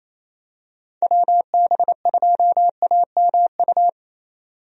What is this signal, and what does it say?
Morse code, heard as a keyed tone, about 28 wpm, 700 Hz. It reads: WB2AMU
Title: USA Call Signs - 6 Characters - 28wpm